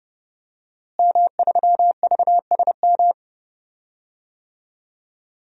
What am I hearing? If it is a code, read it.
Morse code, heard as a keyed tone, about 30 wpm, 700 Hz. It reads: M3VSM
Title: DXCC-100 Call Signs - 5 Characters - Repeated 3x - 30wpm